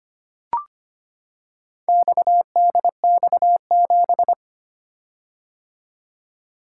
Morse code, heard as a keyed tone, about 25 wpm, 700 Hz. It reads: XDX7
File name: practice / 25wpm / Four Letter-Number Combinations - 25wpm